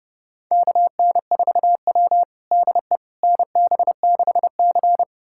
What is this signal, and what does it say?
Morse code, heard as a keyed tone, about 31 wpm, 700 Hz. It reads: KN4W DE NB6C
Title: USA Callsign DE Callsign - 4 Characters - 30wpm